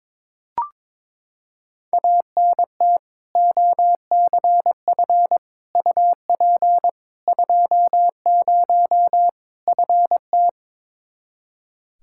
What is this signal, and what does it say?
Morse code, heard as a keyed tone, about 22 wpm, 700 Hz. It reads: ANT OCF UP 20 FT
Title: K1USN SST QSO Practice 22wpm